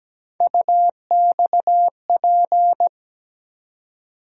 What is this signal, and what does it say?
Morse code, heard as a keyed tone, about 17 wpm, 700 Hz. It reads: UXP
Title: USA Call Sign Postfixes - 3 Letters 17wpm